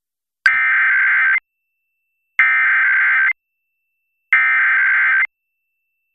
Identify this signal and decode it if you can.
Emergency Alert System (SAME) tones (header bursts), heard as harsh buzzing bursts. It is heard from 0.5 to 5.2 s.